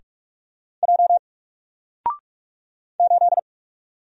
Morse code, heard as a keyed tone, about 44 wpm, 700 Hz. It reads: J 8
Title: Single Letter-Number 45wpm